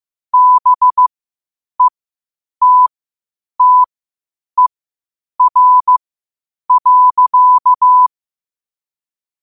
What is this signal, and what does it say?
Morse code, heard as a keyed tone, about 15 wpm, spaced as for 8 wpm, 1000 Hz. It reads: BETTER.
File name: 8wpm